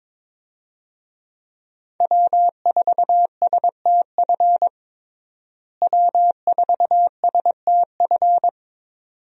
Morse code, heard as a keyed tone, about 22 wpm, 700 Hz. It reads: W4STF W4STF
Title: USA Call Signs - 5 Characters - Repeated 2x - 22wpm